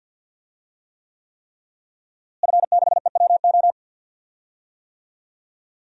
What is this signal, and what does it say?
Morse code, heard as a keyed tone, about 50 wpm, 700 Hz. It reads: F6EWO
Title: DXCC-100 Call Signs - 5 Characters - Repeated 3x - 50wpm